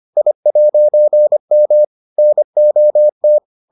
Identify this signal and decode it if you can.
Morse code, heard as a keyed tone, about 25 wpm, 600 Hz. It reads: I'M NOT